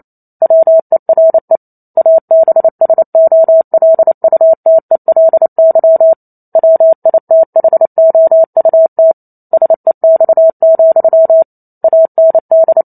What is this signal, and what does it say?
Morse code, heard as a keyed tone, about 28 wpm, 650 Hz. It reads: WERE ABSOLUTELY WITHOUT SEX, AND